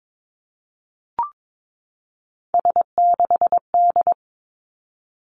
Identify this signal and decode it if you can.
Morse code, heard as a keyed tone, about 22 wpm, 700 Hz. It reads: S6D
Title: Three Challenging Letter-Number Combinations 22wpm